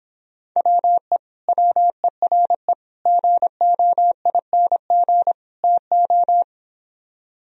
Morse code, heard as a keyed tone, about 26 wpm, 700 Hz. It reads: WE WERE GOING TO